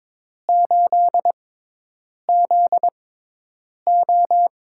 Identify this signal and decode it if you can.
Morse code, heard as a keyed tone, about 22 wpm, spaced as for 8 wpm, 700 Hz. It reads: 8ZO
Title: USA Call Signs - 6 Characters - 6x Character Spacing - 22wpm